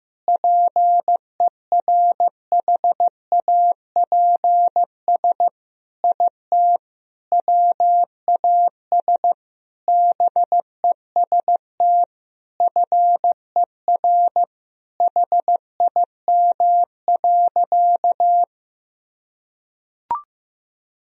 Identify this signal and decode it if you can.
Morse code, heard as a keyed tone, about 15 wpm, 700 Hz. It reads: PERHAPS IT WAS BEST FER HIM.
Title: Sentences from Top 500 Words and Abbreviations 15wpm